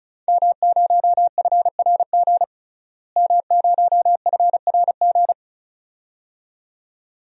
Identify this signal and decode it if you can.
Morse code, heard as a keyed tone, about 34 wpm, 700 Hz. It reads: M0FRG M0FRG
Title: DXCC-100 Call Signs - 5 Characters - Repeated 2x - 35wpm